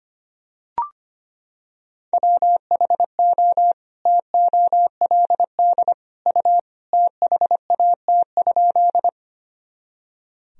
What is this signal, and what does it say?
Morse code, heard as a keyed tone, about 25 wpm, 700 Hz. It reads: WHO TOLD U THAT?
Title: Sentences from Top 400 Words and Abbreviations 25wpm